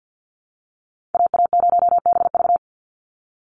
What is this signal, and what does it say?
Morse code, heard as a keyed tone, about 50 wpm, 700 Hz. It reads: AA0BV
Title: NA SOTA Activator Callsigns 50wpm